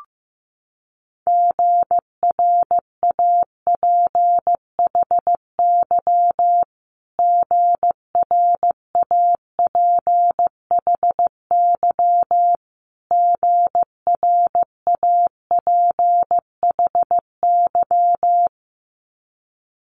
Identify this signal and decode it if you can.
Morse code, heard as a keyed tone, about 15 wpm, 700 Hz. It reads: GRAPHY GRAPHY GRAPHY
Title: Multi-letter Phonemes Repeated 3x 15wpm